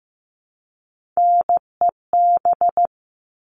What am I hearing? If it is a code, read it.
Morse code, heard as a keyed tone, about 15 wpm, 700 Hz. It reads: NEB
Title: All Cut Numbers - 3-Digit 15wpm